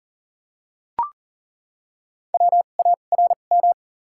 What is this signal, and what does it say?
Morse code, heard as a keyed tone, about 40 wpm, spaced as for 29 wpm, 690 Hz. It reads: WARM
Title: 2x Character Spacing - Top 500 Words - 40wpm